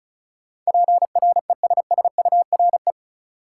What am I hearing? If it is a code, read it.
Morse code, heard as a keyed tone, about 36 wpm, 700 Hz. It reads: PRESSURE
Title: Sentences from Top 1000 Words - Review Part 6 - New Word Drill - 35wpm